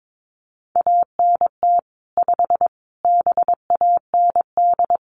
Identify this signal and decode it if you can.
Morse code, heard as a keyed tone, about 22 wpm, 700 Hz. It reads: ANT 5 BAND
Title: Common QSO Phrases 22wpm